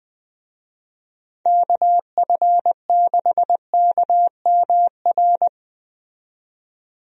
Morse code, heard as a keyed tone, about 20 wpm, 700 Hz. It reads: KF6KMR
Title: USA Call Signs - 6 Characters - 20wpm